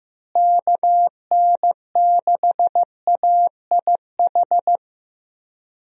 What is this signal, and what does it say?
Morse code, heard as a keyed tone, about 15 wpm, 700 Hz. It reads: KN6AIH
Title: USA Call Signs - 6 Characters - Repeated 2x - 15wpm